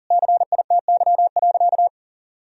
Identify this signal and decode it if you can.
Morse code, heard as a keyed tone, about 40 wpm, 700 Hz. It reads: CITY.
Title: Sentences from Top 1000 Words - Review Part 1 - 40wpm